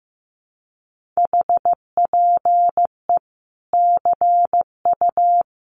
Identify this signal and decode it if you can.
Morse code, heard as a keyed tone, about 15 wpm, 700 Hz. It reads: HPE CU
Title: Common QSO Phrases 15wpm